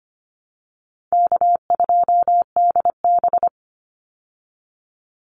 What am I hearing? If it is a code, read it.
Morse code, heard as a keyed tone, about 25 wpm, 700 Hz. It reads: K2DB
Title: Easy Call Signs Repeated 3x - 25wpm